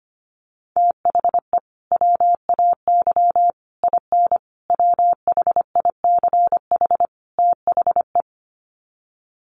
Morse code, heard as a keyed tone, about 25 wpm, 700 Hz. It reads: THE WAY IN WHICH THE
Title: Sets of 5 Words Encore2 25wpm